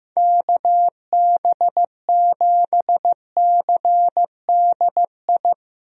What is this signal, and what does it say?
Morse code, heard as a keyed tone, about 15 wpm, 700 Hz. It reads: KB7CDI
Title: USA Call Signs - 6 Characters - Repeated 3x - 15wpm